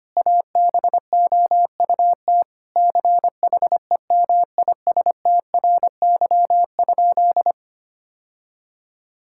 Morse code, heard as a keyed tone, about 25 wpm, 700 Hz. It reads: ABOUT CHEMISTRY?
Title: Sentences from Top 1000 Words - Review Part 6 - 25wpm